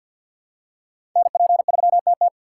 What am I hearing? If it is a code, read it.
Morse code, heard as a keyed tone, about 52 wpm, 700 Hz. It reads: NP3TT
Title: USA Call Signs - 5 Characters - 2x Character Spacing - 50wpm